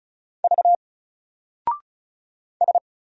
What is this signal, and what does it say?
Morse code, heard as a keyed tone, about 34 wpm, 700 Hz. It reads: V S
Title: Single Character (Letter:Number:Punctuation:Prosign) 35wpm